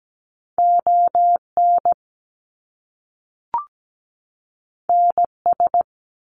Morse code, heard as a keyed tone, about 17 wpm, 700 Hz. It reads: ON NS
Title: Canadian Provinces and Territories 17wpm